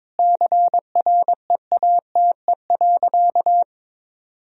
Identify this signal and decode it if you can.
Morse code, heard as a keyed tone, about 22 wpm, 700 Hz. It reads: CREATE.
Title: Sentences from Top 1000 Words - Review Part 3 - 22wpm